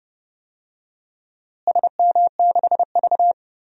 Morse code, heard as a keyed tone, about 30 wpm, 700 Hz. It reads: SM6V
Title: DXCC-100 Call Signs - 4 Characters - Repeated 3x - 30wpm